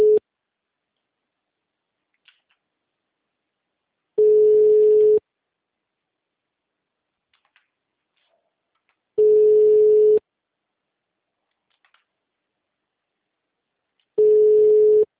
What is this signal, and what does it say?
Ringback tone (Europe, 425 Hz).
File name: out- -2011- - - .86566